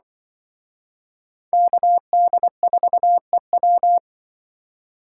Morse code, heard as a keyed tone, about 25 wpm, 700 Hz. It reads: KD4EW
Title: Speed Racing - USA 2x2 Call Signs 20wpm